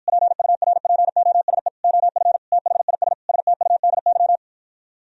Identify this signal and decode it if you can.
Morse code, heard as a keyed tone, about 52 wpm, 700 Hz. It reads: PURPOSE OF THIS STUDY